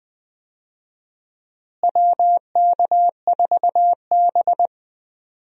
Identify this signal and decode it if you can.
Morse code, heard as a keyed tone, about 20 wpm, 700 Hz. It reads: WK4B
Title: USA Call Signs - 4 Characters - Repeated 3x - 20wpm